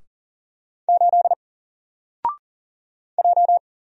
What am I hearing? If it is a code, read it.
Morse code, heard as a keyed tone, about 39 wpm, 700 Hz. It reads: 8 J
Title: Single Letter-Number Alt 40wpm